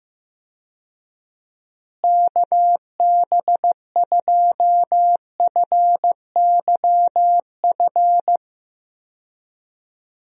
Morse code, heard as a keyed tone, about 15 wpm, 700 Hz. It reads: KB2FYF
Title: USA Call Signs - 6 Characters - Repeated 3x - 15wpm